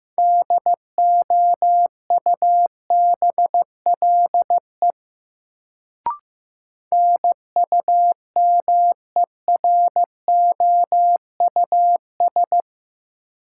Morse code, heard as a keyed tone, about 15 wpm, 700 Hz. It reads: DOUBLE NUMEROUS
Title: Sentences from Top 1400 Words - New Word Drill - 15wpm